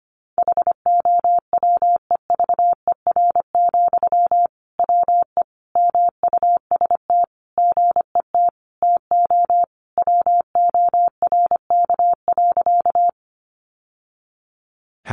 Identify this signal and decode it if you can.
Morse code, heard as a keyed tone, about 25 wpm, 700 Hz. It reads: HOWEVER, WE MUST GET TO WORK.
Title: Sentences from Top 600 Words 25wpm